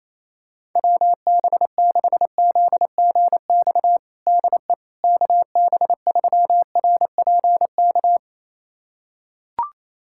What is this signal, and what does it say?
Morse code, heard as a keyed tone, about 28 wpm, 700 Hz. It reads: WB6ZGX DE KB3RPK
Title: USA Callsign DE Callsign 28wpm